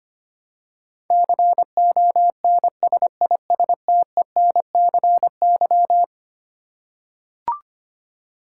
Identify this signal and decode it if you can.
Morse code, heard as a keyed tone, about 25 wpm, 700 Hz. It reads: CONSISTENCY
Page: morse-code-ninja-practice-25wpm